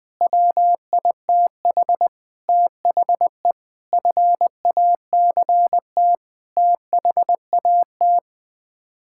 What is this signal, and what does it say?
Morse code, heard as a keyed tone, about 21 wpm, 700 Hz. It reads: WITH THE FACT THAT
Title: Sets of 8 Words - 20wpm